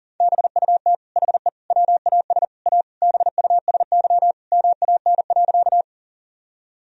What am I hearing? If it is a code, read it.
Morse code, heard as a keyed tone, about 37 wpm, 700 Hz. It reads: BUT HE WAS A BUSY MAN.
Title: Sentences from Top 1000 Words - Review Part 3 - 40wpm